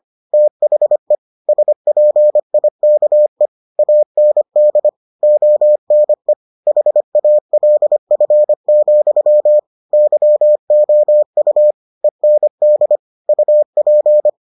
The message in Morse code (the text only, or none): THE SPIKE AND ONE HALF, YOU END UP